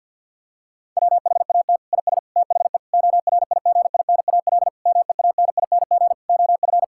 Morse code, heard as a keyed tone, about 50 wpm, 700 Hz. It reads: WHAT IS THE ORIGINAL MEANING OF
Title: Sentences from Top 1000 Words - Review Part 6 - 50wpm